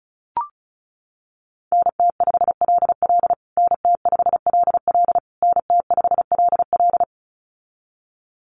Morse code, heard as a keyed tone, about 34 wpm, 700 Hz. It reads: NT5LL NT5LL NT5LL
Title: USA Call Signs Repeated 3x 35wpm